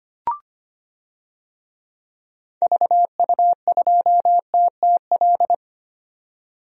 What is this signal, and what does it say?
Morse code, heard as a keyed tone, about 25 wpm, 700 Hz. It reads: VU2TTL
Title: DXCC-100 Call Signs - 6 Characters - 25wpm